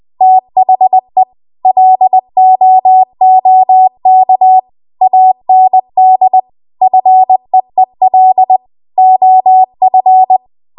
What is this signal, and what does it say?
Morse code, heard as a keyed tone, about 20 wpm, 750 Hz. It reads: THE LOOK AND FEEL OF
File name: Archive / 20 WPM / 20WPM